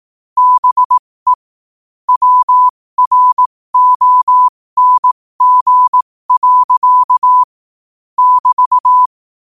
Morse code, heard as a keyed tone, about 18 wpm, spaced as for 15 wpm, 1000 Hz. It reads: BE WRONG. <BT>
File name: qotdfiles / 15wpm